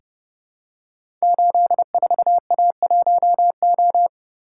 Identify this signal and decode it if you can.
Morse code, heard as a keyed tone, about 30 wpm, 700 Hz. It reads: 84A1O
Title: Code Groups - 5 Letter:Numbers - 30wpm